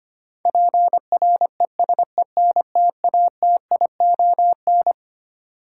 Morse code, heard as a keyed tone, about 25 wpm, 700 Hz. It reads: PRESENTATION
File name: Twelve Letter Words - 25wpm